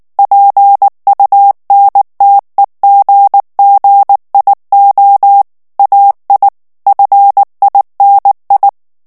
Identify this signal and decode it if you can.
Morse code, heard as a keyed tone, about 19 wpm, 800 Hz. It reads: PUNTEGGIO AI FINI
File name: morse mp3 / ITALIEN texte clair 2 vitesse 18 mots minute